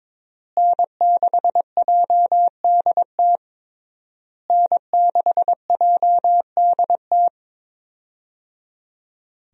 Morse code, heard as a keyed tone, about 22 wpm, 700 Hz. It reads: N6JDT N6JDT
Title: USA Call Signs - 5 Characters - Repeated 3x - 22wpm